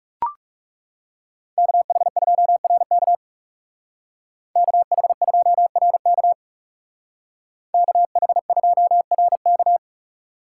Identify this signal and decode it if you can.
Morse code, heard as a keyed tone, about 40 wpm, 700 Hz. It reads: KH2RK KH2RK KH2RK